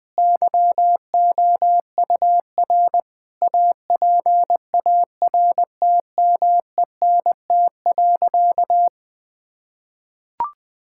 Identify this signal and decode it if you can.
Morse code, heard as a keyed tone, about 20 wpm, 700 Hz. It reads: YOUR APARTMENT.
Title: Sentences from Top 1000 Words - Review Part 6 - 20wpm